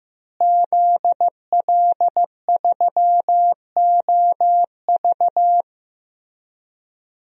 Morse code, heard as a keyed tone, about 16 wpm, 700 Hz. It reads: ZL3OV
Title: DXCC-100 Call Signs - 5 Characters - Repeated 3x - 15wpm